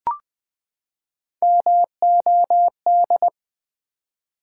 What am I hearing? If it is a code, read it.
Morse code, heard as a keyed tone, about 20 wpm, 700 Hz. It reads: MOD
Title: Build A Word - 3 Letters At A Time - 8-Letter Words - 20wpm